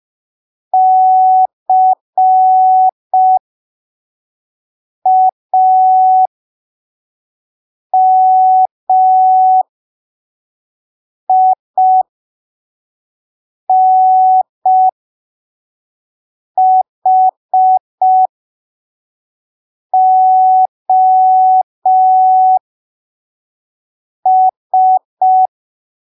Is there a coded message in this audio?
Yes: Morse code from 0.7 to 25.5 s, heard as a keyed tone, about 5 wpm, spaced as for 3 wpm, 750 Hz. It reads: CAMINHOS